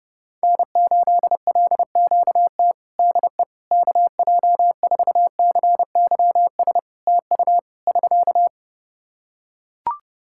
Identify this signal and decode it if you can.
Morse code, heard as a keyed tone, about 30 wpm, 700 Hz. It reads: N8LQT DE KJ4CYS TU <SK>
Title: K1USN SST QSO Practice 30wpm